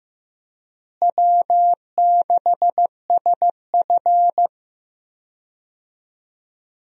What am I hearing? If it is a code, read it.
Morse code, heard as a keyed tone, about 15 wpm, 700 Hz. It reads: W6SF
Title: USA Call Signs - 4 Characters - Repeated 3x - 15wpm